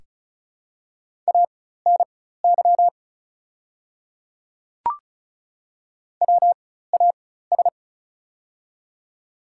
Morse code, heard as a keyed tone, about 36 wpm, spaced as for 16 wpm, 700 Hz. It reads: ANY WAS
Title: 4x Character Spacing - Top 200 Words - 35wpm